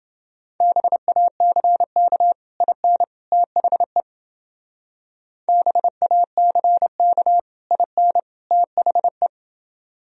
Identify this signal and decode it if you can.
Morse code, heard as a keyed tone, about 29 wpm, 700 Hz. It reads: BACK IN THE BACK IN THE